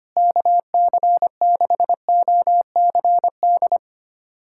Morse code, heard as a keyed tone, about 25 wpm, 700 Hz. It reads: KC6OCD
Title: USA Callsign DE Callsign - 6 Characters - 25wpm